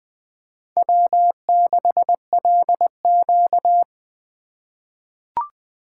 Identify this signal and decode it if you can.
Morse code, heard as a keyed tone, about 20 wpm, 700 Hz. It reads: W6LQ
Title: USA Call Signs - 4 Characters - 3x Character Spacing - 20wpm